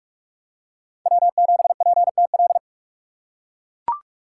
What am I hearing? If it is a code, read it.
Morse code, heard as a keyed tone, about 43 wpm, 700 Hz. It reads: W7PTL
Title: USA Call Signs - 6x Character Spacing - 45wpm